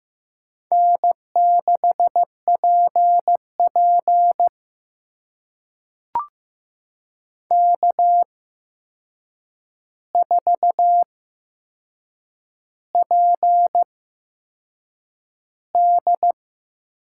Morse code, heard as a keyed tone, about 15 wpm, spaced as for 6 wpm, 700 Hz. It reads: N6PP K4PD
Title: USA Call Signs - 4 Characters - 8x Character Spacing - 15wpm